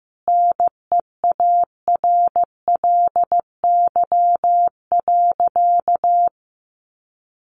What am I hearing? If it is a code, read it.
Morse code, heard as a keyed tone, about 15 wpm, 700 Hz. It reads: NEARLY.
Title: Sentences from Top 1300 Words 15wpm